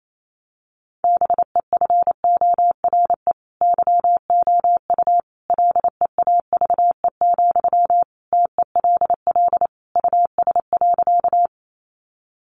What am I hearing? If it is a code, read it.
Morse code, heard as a keyed tone, about 28 wpm, 700 Hz. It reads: BEFORE YOU LEAVE, TELL US.
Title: Sentences from Top 400 Words 28wpm